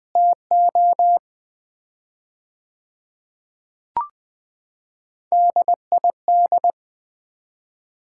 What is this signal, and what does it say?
Morse code, heard as a keyed tone, about 20 wpm, 700 Hz. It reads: TO DID
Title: 4x Word-Spacing - Sets of 4 Words - 20wpm